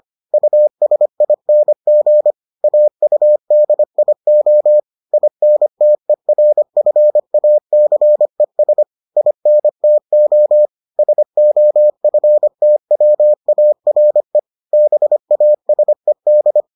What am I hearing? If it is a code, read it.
Morse code, heard as a keyed tone, about 25 wpm, 600 Hz. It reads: USING AUDIO INTERFACES INTO SOFTWARE BASED